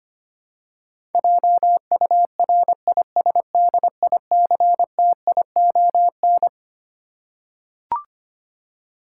Morse code, heard as a keyed tone, about 25 wpm, 700 Hz. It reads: JURISDICTION
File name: Twelve Letter Words - 25wpm